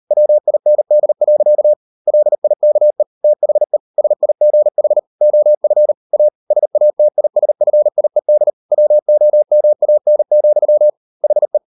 Morse code, heard as a keyed tone, about 38 wpm, 600 Hz. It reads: WIND. LIKE THE SIGH OF A SATISFIED WOMAN, HE